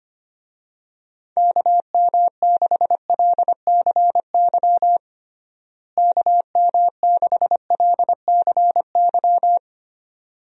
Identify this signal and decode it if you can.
Morse code, heard as a keyed tone, about 25 wpm, 700 Hz. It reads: KM6LCY KM6LCY
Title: USA Call Signs - 6 Characters - Repeated 2x - 25wpm